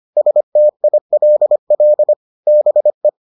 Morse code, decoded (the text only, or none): STILL BE